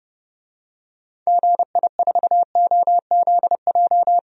Morse code, heard as a keyed tone, about 32 wpm, 700 Hz. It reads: GI4OZJ
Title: DXCC-100 Call Signs - 6 Characters - 30wpm